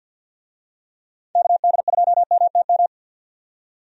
Morse code, heard as a keyed tone, about 52 wpm, 700 Hz. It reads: KD2MTM
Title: USA Call Signs - 6 Characters - 50wpm